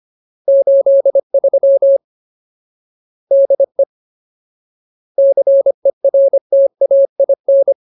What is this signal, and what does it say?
Morse code, heard as a keyed tone, about 25 wpm, 550 Hz. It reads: 83 DE CERTAIN
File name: Ultimate Word List / 25WPM Ultimate